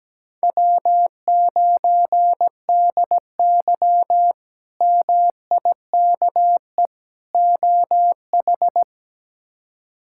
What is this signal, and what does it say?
Morse code, heard as a keyed tone, about 17 wpm, 700 Hz. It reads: W9DY MIKE OH
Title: K1USN SST QSO Practice - 17wpm